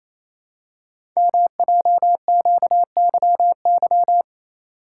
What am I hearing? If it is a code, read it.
Morse code, heard as a keyed tone, about 27 wpm, 700 Hz. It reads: MJQYY
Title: Code Groups - 5 Letters - 28wpm